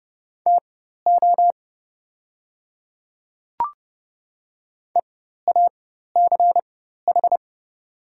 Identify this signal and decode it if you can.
Morse code, heard as a keyed tone, about 30 wpm, spaced as for 14 wpm, 700 Hz. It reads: TO EACH
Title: 4x Character Spacing - Top 100 Words - 30wpm